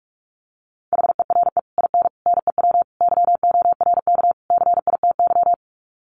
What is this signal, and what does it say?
Morse code, heard as a keyed tone, about 44 wpm, 700 Hz. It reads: HERE IN NEW YORK CITY
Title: Sets of 5 Words Encore2 45wpm